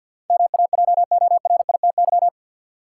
Morse code, heard as a keyed tone, about 52 wpm, 700 Hz. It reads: MAJORITY
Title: Sets of 3 Words - New Word Drill - 50wpm